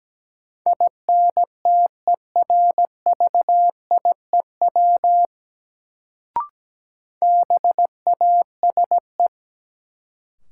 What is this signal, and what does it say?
Morse code, heard as a keyed tone, about 17 wpm, 700 Hz. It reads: INTERVIEW BASE
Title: Sentences from Top 1600 Words - New Word Drill Alt - 17wpm